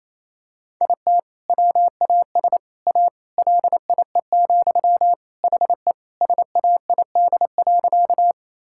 Morse code, heard as a keyed tone, about 28 wpm, 700 Hz. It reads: IT WAS A LIE, HE SAID.
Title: Sentences from Top 1600 Words 28wpm